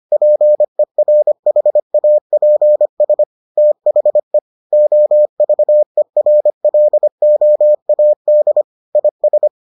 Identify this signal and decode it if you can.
Morse code, heard as a keyed tone, about 25 wpm, 600 Hz. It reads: PERHAPS THE OVERLOAD IS